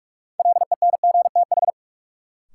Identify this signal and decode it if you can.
Morse code, heard as a keyed tone, about 45 wpm, 700 Hz. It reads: LENGTH
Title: Sentences from Top 1000 Words - New Word Drill - 45wpm